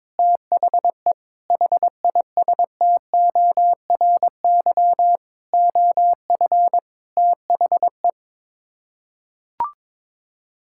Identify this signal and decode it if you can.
Morse code, heard as a keyed tone, about 22 wpm, 700 Hz. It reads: THE HISTORY OF THE
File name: Sets of 9 Words - 22wpm